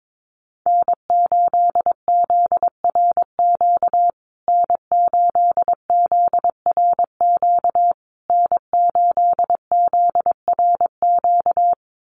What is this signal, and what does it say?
Morse code, heard as a keyed tone, about 22 wpm, 700 Hz. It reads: N8ZRQ N8ZRQ N8ZRQ
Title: USA Call Signs Repeated 3x 22wpm